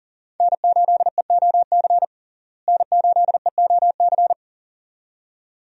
Morse code, heard as a keyed tone, about 40 wpm, 700 Hz. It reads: N8EOC N8EOC
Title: USA Call Signs - 5 Characters - Repeated 2x - 40wpm